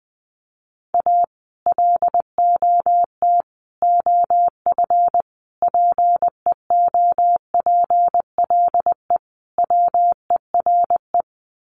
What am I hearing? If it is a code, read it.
Morse code, heard as a keyed tone, about 20 wpm, 700 Hz. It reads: A LOT OF PEOPLE WERE
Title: Sets of 5 Words Encore2 20wpm